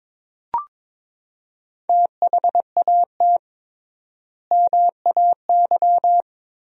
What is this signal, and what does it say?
Morse code, heard as a keyed tone, about 22 wpm, 700 Hz. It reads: THAT MAY
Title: 3x Word-Spacing - Sentences from Top 400 Words - 22wpm